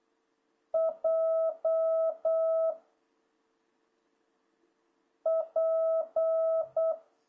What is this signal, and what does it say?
Morse code, heard as a keyed tone, about 8 wpm, 660 Hz. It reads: J P
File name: jh8cbh / 2026cw